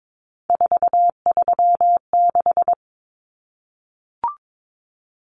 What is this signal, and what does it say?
Morse code, heard as a keyed tone, about 22 wpm, 700 Hz. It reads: 436
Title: Common Cut Numbers - 3-Digit 22wpm